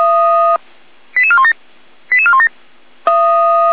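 Five-tone selective call (ZVEI), heard as quick runs of five tones, tones of about 70 ms. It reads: 88417 88417
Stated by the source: ZVEI five-tone selective call received on an SDR.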